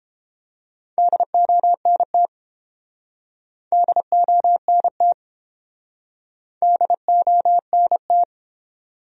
Morse code, heard as a keyed tone, about 32 wpm, 700 Hz. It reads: DONT DONT DONT